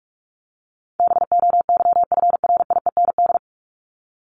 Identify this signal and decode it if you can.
Morse code, heard as a keyed tone, about 44 wpm, 700 Hz. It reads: BOYFRIEND
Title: Top 501-750 Words 45wpm